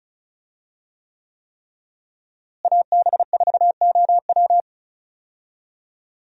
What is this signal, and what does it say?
Morse code, heard as a keyed tone, about 36 wpm, 700 Hz. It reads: AB4OW
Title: USA Call Signs - 5 Characters - Repeated 3x - 35wpm